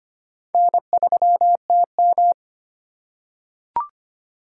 Morse code, heard as a keyed tone, about 25 wpm, 700 Hz. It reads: N3TM
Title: USA Call Signs - 4 Characters - 4x Character Spacing - 25wpm